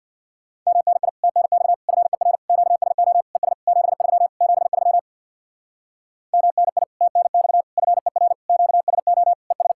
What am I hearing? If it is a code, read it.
Morse code, heard as a keyed tone, about 66 wpm, 700 Hz. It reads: MNI TNX FER QSO ES 73 73 MNI TNX FER QSO ES